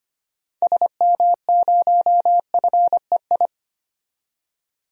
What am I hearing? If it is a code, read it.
Morse code, heard as a keyed tone, about 25 wpm, 700 Hz. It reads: SM0FEI
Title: DXCC-100 Call Signs - 6 Characters - 25wpm